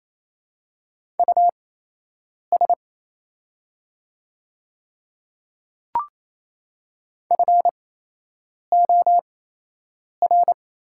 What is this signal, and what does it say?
Morse code, heard as a keyed tone, about 27 wpm, spaced as for 8 wpm, 700 Hz. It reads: US FOR
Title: 8x Character Spacing - Top 300 Words - 28wpm